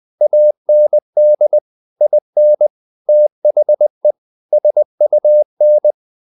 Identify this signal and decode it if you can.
Morse code, heard as a keyed tone, about 20 wpm, 600 Hz. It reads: AND IN THE SUN
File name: the old man and the sea / the old man and the sea part 1 20wpm